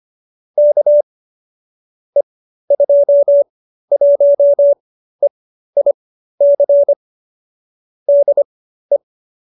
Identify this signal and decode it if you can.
Morse code, heard as a keyed tone, about 25 wpm, spaced as for 13 wpm, 570 Hz. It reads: K E21EIC DE
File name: qso208b 13